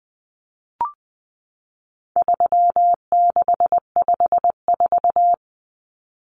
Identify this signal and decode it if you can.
Morse code, heard as a keyed tone, about 20 wpm, 700 Hz. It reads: 3654
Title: Common Cut Numbers - 4-Digit 20wpm